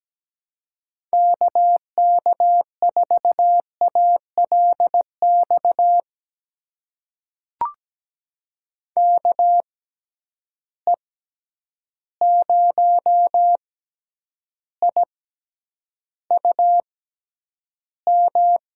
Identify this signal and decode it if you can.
Morse code, heard as a keyed tone, about 17 wpm, spaced as for 9 wpm, 700 Hz. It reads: KK4ALX KE0IUM
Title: USA Call Signs - 6 Characters - 6x Character Spacing - 17wpm